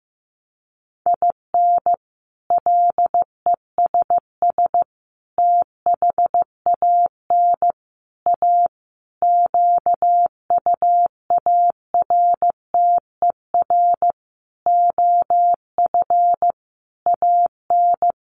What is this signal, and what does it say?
Morse code, heard as a keyed tone, about 14 wpm, 700 Hz. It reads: IN LESS THAN A QUARTER OF AN
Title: Sentences from Top 1300 Words 15wpm